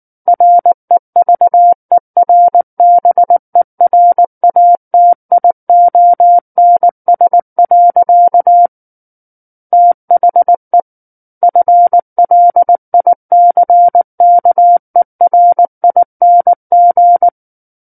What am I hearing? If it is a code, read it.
Morse code, heard as a keyed tone, about 19 wpm, 700 Hz. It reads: REVERBERATIONS. THE FLICKERING
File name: Book One / 10-Chapter 10 - 19 WPM